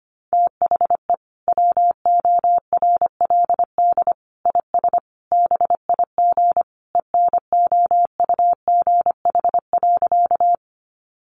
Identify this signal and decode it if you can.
Morse code, heard as a keyed tone, about 25 wpm, 700 Hz. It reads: THE WORLD IS BIG ENOUGH.
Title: Sentences from Top 400 Words 25wpm